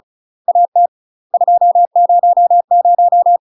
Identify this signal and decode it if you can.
Morse code, heard as a keyed tone, about 34 wpm, 700 Hz. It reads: AT 200
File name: UK MASTER QSOs TRAINING / 35WPM / UK MASTER QSOs 0005